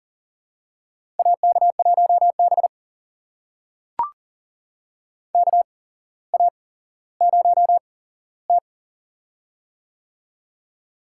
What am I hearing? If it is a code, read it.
Morse code, heard as a keyed tone, about 40 wpm, spaced as for 17 wpm, 700 Hz. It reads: AK1B KA0T
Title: USA Call Signs - 4 Characters - 8x Character Spacing - 40wpm